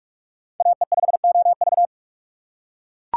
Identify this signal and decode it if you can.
Morse code, heard as a keyed tone, about 40 wpm, 700 Hz. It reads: AE5OV